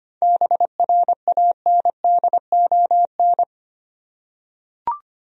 Morse code, heard as a keyed tone, about 25 wpm, 700 Hz. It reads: BRANDON
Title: USA Common Names - 25wpm